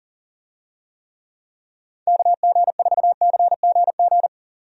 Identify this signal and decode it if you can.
Morse code, heard as a keyed tone, about 40 wpm, 700 Hz. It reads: KG4CGG
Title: USA Call Signs - 6 Characters - 40wpm